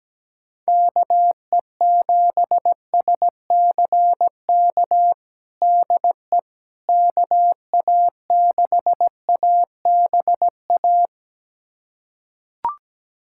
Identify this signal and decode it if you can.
Morse code, heard as a keyed tone, about 17 wpm, 700 Hz. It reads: KE7SCK DE KA6ABA
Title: USA Callsign DE Callsign - 6 Characters - 17wpm